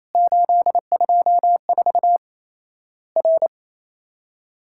Morse code, heard as a keyed tone, about 28 wpm, 700 Hz. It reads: 824 R
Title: Worked All Europe DX Contest - QTC Exchange - 28wpm